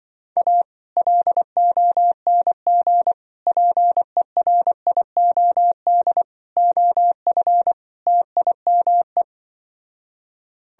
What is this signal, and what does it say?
Morse code, heard as a keyed tone, about 25 wpm, 700 Hz. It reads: A LONG PERIOD OF TIME
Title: Speed-Racing - Sets of 5 Words 20wpm